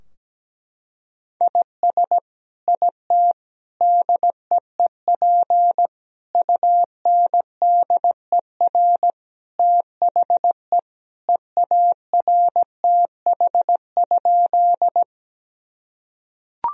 Morse code, heard as a keyed tone, about 17 wpm, 700 Hz. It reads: IS IT DEEP UNDER THE EARTH?
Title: Sentences from Top 1000 Words - Review Part 3 - 17wpm